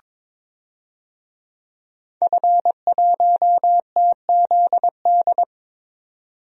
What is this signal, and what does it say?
Morse code, heard as a keyed tone, about 22 wpm, 700 Hz. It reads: F1TZD
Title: DXCC-100 Call Signs - 5 Characters - 22wpm